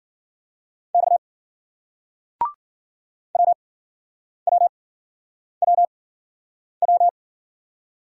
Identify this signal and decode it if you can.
Morse code, heard as a keyed tone, about 58 wpm, spaced as for 9 wpm, 700 Hz. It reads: X WWWW